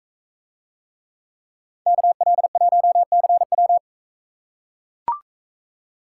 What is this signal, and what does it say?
Morse code, heard as a keyed tone, about 40 wpm, 700 Hz. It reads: KL1CW